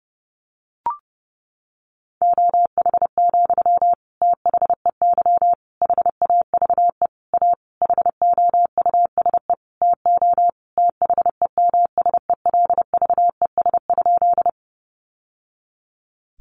Morse code, heard as a keyed tone, about 30 wpm, 700 Hz. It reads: OH, THEY HAVE A HOUSE TO THEMSELVES?
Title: Sentences from Top 800 Words 30wpm